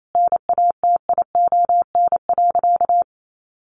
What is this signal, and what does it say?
Morse code, heard as a keyed tone, about 28 wpm, 700 Hz. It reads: NATION.
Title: A Sound Of Thunder 28wpm